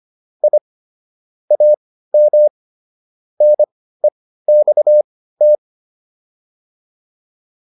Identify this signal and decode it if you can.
Morse code, heard as a keyed tone, about 25 wpm, spaced as for 15 wpm, 600 Hz. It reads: I AM NEXT